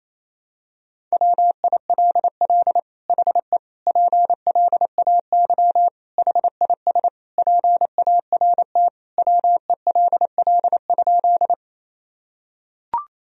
Morse code, heard as a keyed tone, about 28 wpm, 700 Hz. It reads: WILL HE PLAY HIS PART WELL?
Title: Sentences from Top 1000 Words - Review Part 1 - 28wpm